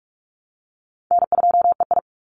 Morse code, heard as a keyed tone, about 44 wpm, 700 Hz. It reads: N2EI
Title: Easy Call Signs - 45wpm